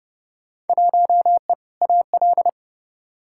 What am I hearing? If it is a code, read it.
Morse code, heard as a keyed tone, about 29 wpm, 700 Hz. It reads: 1E AL
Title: Field Day Exchanges - 30wpm